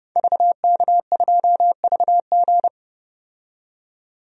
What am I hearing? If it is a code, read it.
Morse code, heard as a keyed tone, about 30 wpm, 700 Hz. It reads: VK2VG
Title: DXCC-100 Call Signs - 5 Characters - 30wpm